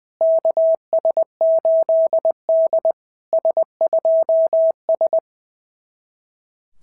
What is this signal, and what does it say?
Morse code, heard as a keyed tone, about 20 wpm, 660 Hz. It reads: KS8D S2S
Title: SOTA QSO Practice 20wpm